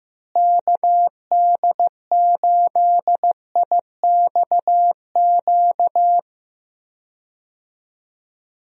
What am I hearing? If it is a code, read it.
Morse code, heard as a keyed tone, about 15 wpm, 700 Hz. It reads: KD8IXQ
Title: USA Call Signs - 6 Characters - Repeated 2x - 15wpm